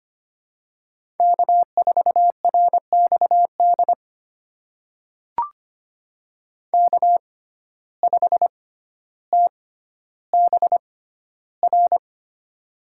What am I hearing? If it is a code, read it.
Morse code, heard as a keyed tone, about 25 wpm, spaced as for 13 wpm, 700 Hz. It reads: K4RXD K5TBR
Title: USA Call Signs - 5 Characters - 6x Character Spacing - 25wpm